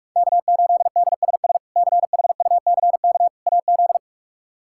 Morse code, heard as a keyed tone, about 46 wpm, 700 Hz. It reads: K8DSS CHUCK AZ
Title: K1USN SST QSO Practice - 45wpm